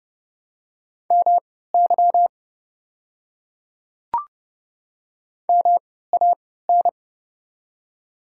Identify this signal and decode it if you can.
Morse code, heard as a keyed tone, about 29 wpm, spaced as for 17 wpm, 700 Hz. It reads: MY MAN
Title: 3x Character Spacing - Top 500 Words - 30wpm